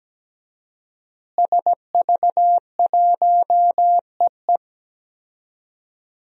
Morse code, heard as a keyed tone, about 17 wpm, 700 Hz. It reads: SV1EE
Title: DXCC-100 Call Signs - 5 Characters - Repeated 2x - 17wpm